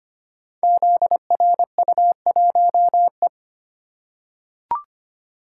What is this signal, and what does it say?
Morse code, heard as a keyed tone, about 25 wpm, 700 Hz. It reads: ZRU1E
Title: Code Groups - 5 Letter:Numbers - 25wpm